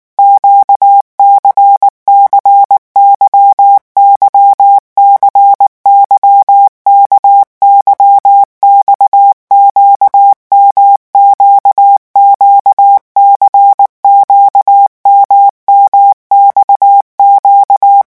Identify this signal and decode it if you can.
Morse code, heard as a keyed tone, about 19 wpm, 800 Hz. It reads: QCCYY CYKYX QMQQC QMMXQ
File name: revision1-vitesse 18 mots